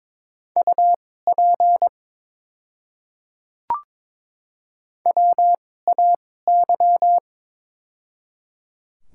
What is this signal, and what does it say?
Morse code, heard as a keyed tone, about 22 wpm, spaced as for 16 wpm, 700 Hz. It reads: UP WAY
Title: 2x Character Spacing - Top 100 Words - 22wpm